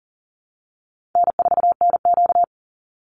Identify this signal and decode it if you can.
Morse code, heard as a keyed tone, about 40 wpm, 700 Hz. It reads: N4NQ
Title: Call Signs Repeated 2x - 40wpm